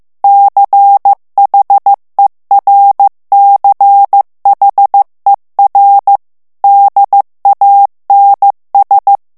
Morse code, heard as a keyed tone, about 15 wpm, 800 Hz. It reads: CHERCHER DANS
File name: morse mp3 / FRANCAIS TEXTE CLAIR 2 VITESSE 14 MOTS MINUTES